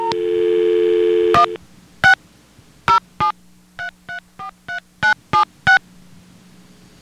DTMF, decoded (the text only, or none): 1 6 073343976